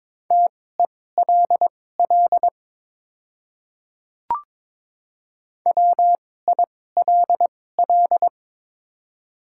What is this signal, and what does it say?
Morse code, heard as a keyed tone, about 22 wpm, spaced as for 16 wpm, 700 Hz. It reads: TELL WILL
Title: 2x Character Spacing - Top 300 Words - 22wpm